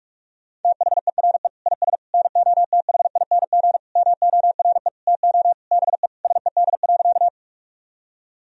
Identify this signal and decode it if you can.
Morse code, heard as a keyed tone, about 46 wpm, 690 Hz. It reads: THERE IS NOTHING MORE TO BE SED.
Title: Sentences from Top 500 Words and Abbreviations 45wpm